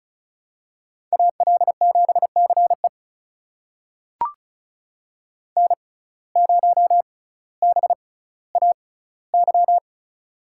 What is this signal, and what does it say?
Morse code, heard as a keyed tone, about 36 wpm, spaced as for 18 wpm, 700 Hz. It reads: AL7CE N0BAY
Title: USA Call Signs - 5 Characters - 6x Character Spacing - 35wpm